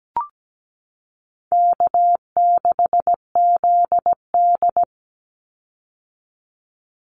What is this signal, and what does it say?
Morse code, heard as a keyed tone, about 17 wpm, 700 Hz. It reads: K6ZD
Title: Easy Call Signs - 17wpm